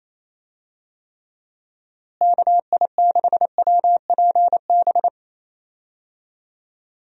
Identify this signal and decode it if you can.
Morse code, heard as a keyed tone, about 28 wpm, 700 Hz. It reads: KI6WPB
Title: USA Call Signs - 6 Characters - Repeated 3x - 28wpm